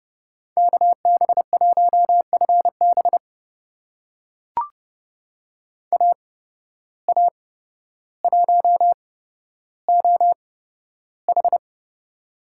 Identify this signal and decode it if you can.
Morse code, heard as a keyed tone, about 30 wpm, spaced as for 13 wpm, 700 Hz. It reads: KB1FB AA1OH
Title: USA Call Signs - 5 Characters - 8x Character Spacing - 30wpm